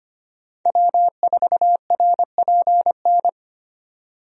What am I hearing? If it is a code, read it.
Morse code, heard as a keyed tone, about 25 wpm, 700 Hz. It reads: W4RPN
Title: USA Call Signs - 5 Characters - 25wpm